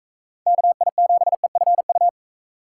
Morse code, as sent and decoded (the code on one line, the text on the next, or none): -.- .. --... ..-. ..-
KI7EFU